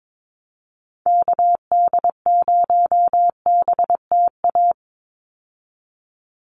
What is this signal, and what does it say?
Morse code, heard as a keyed tone, about 22 wpm, 700 Hz. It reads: KD0BTA
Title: USA Call Signs 22wpm